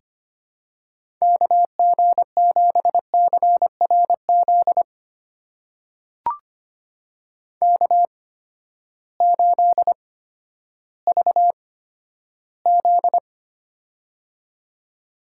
Morse code, heard as a keyed tone, about 25 wpm, spaced as for 12 wpm, 700 Hz. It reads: KG7CRZ K8VZ
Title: USA Call Signs - 8x Character Spacing - 25wpm